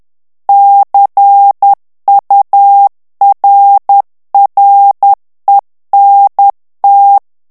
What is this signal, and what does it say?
Morse code, heard as a keyed tone, about 10 wpm, 800 Hz. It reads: CURRENT
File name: ANGLAIS texte clair 3 vitesse 10 mots minute